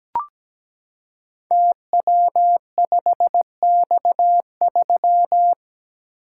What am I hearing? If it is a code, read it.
Morse code, heard as a keyed tone, about 17 wpm, 700 Hz. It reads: TW5X3
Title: Code Groups - 5 Letter:Numbers - 17wpm